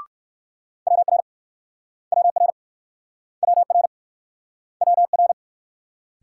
Morse code, heard as a keyed tone, about 52 wpm, 700 Hz. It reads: WR WR WR WR
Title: Speed-Racing Multi-letter Phonemes - 45wpm